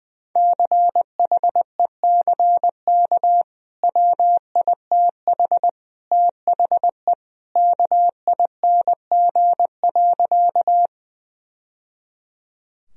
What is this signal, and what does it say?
Morse code, heard as a keyed tone, about 20 wpm, 700 Hz. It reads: CHECK WITH THE KING.
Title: Sentences from Top 1000 Words - Review Part 3 - 20wpm